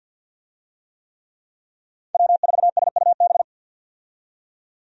Morse code, heard as a keyed tone, about 45 wpm, 700 Hz. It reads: W4SUB
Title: USA Call Signs - 5 Characters - 50wpm